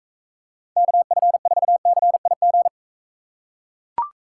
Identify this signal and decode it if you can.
Morse code, heard as a keyed tone, about 42 wpm, 700 Hz. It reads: KF4CIG